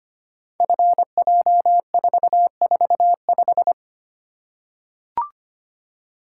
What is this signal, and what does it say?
Morse code, heard as a keyed tone, about 25 wpm, 700 Hz. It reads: FJ445
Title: Code Groups - 5 Letter:Numbers - 25wpm